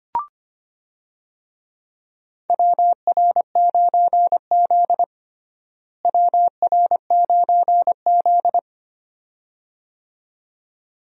Morse code, heard as a keyed tone, about 25 wpm, 700 Hz. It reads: WR9Z WR9Z
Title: USA Call Signs - 4 Characters - Repeated 2x - 25wpm